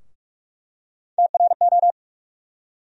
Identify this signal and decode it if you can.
Morse code, heard as a keyed tone, about 43 wpm, 700 Hz. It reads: TRO
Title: USA Call Sign Postfixes - 3 Letters 45wpm